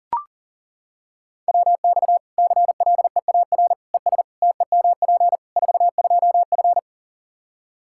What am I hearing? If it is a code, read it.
Morse code, heard as a keyed tone, about 40 wpm, 700 Hz. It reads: WX CLEAR ES TEMP 42F
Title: Practice QSO - Exchange 2 - Info - 40wpm